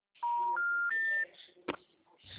Special information tone.
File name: out- -2024- - - .2